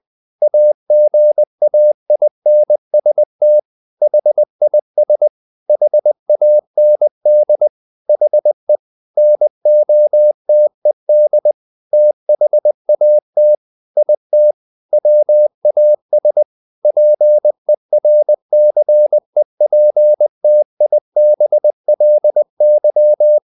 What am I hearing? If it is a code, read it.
Morse code, heard as a keyed tone, about 20 wpm, 600 Hz. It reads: AGAINST HIS HAND HE NOTED THAT IT WAS PERCEPTIBLY
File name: the old man and the sea part 2 20wpm